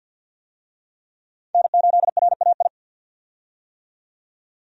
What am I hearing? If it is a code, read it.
Morse code, heard as a keyed tone, about 48 wpm, 700 Hz. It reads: N8RAI
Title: USA Call Signs - 5 Characters - 50wpm